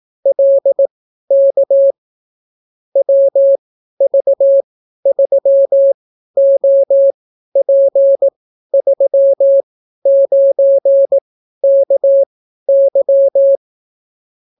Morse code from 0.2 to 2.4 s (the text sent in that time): LK